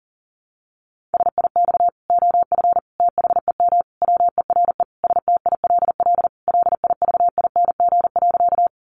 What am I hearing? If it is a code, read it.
Morse code, heard as a keyed tone, about 40 wpm, 700 Hz. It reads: SIX OF THEM WERE STILL LIVING.
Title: Sentences from Top 1100 Words 40wpm